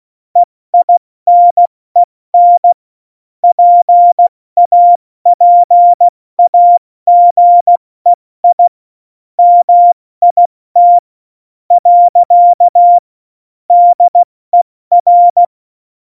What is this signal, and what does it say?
Morse code, heard as a keyed tone, about 16 wpm, 700 Hz. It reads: EINEN PAPAGEI MIT . DER
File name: Morsekurs / Witze / Witz 04. Dezember, 70 F80